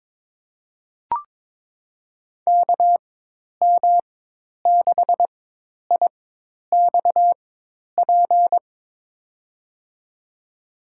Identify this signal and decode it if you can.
Morse code, heard as a keyed tone, about 21 wpm, spaced as for 10 wpm, 700 Hz. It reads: KM6IXP
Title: USA Call Signs - 6 Characters - 4x Character Spacing - 22wpm